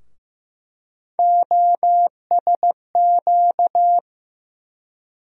Morse code, heard as a keyed tone, about 15 wpm, 700 Hz. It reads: OSQ
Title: USA Call Sign Postfixes - 3 Letters 15wpm